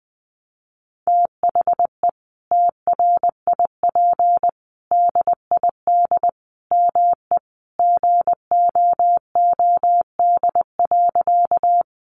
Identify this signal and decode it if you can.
Morse code, heard as a keyed tone, about 20 wpm, 700 Hz. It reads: THE TRIP DID ME GOOD.
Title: Sentences from Top 1300 Words 20wpm